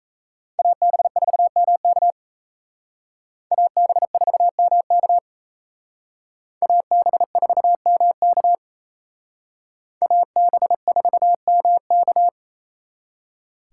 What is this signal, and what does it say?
Morse code, heard as a keyed tone, about 34 wpm, 700 Hz. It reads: AB4MK AB4MK AB4MK AB4MK
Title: Speed Racing - USA Call Signs 28wpm